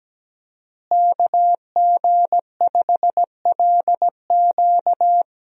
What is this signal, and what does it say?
Morse code, heard as a keyed tone, about 17 wpm, 700 Hz. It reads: KG5LQ
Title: USA Call Signs - 5 Characters - Repeated 3x - 17wpm